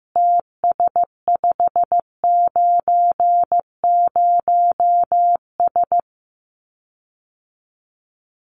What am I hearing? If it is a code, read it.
Morse code, heard as a keyed tone, about 15 wpm, 700 Hz. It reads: TS590S
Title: Common QSO Phrases Encore 15wpm